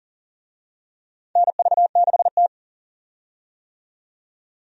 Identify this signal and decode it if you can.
Morse code, heard as a keyed tone, about 40 wpm, 700 Hz. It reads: NV6T
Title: USA Call Signs - 4 Characters - 40wpm